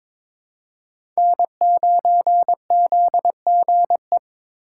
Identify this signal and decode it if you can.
Morse code, heard as a keyed tone, about 22 wpm, 700 Hz. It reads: N9ZGE
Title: USA Call Signs - 5 Characters - 2x Character Spacing - 22wpm